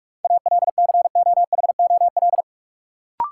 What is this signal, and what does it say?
Morse code, heard as a keyed tone, about 41 wpm, 700 Hz. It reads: ALCOHOL
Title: Sentences from Top 1400 Words - New Word Drill - 45wpm